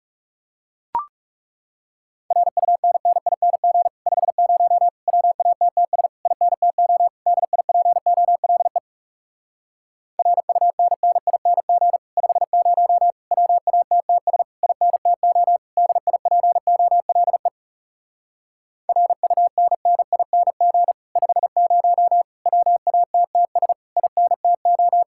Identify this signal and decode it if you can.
Morse code, heard as a keyed tone, about 40 wpm, 700 Hz. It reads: RUNNING 50 WATTS INTO DIPOLE RUNNING 50 WATTS INTO DIPOLE RUNNING 50 WATTS INTO